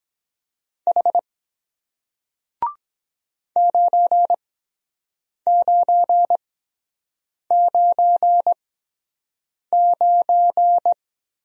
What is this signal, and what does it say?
Morse code, heard as a keyed tone, about 23 wpm, spaced as for 7 wpm, 700 Hz. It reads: H 9999